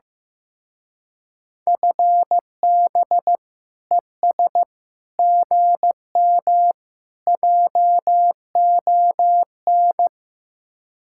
Morse code, heard as a keyed tone, about 15 wpm, 700 Hz. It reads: FB ES GM JON
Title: Speed-Racing Common QSO Phrases Encore - 15wpm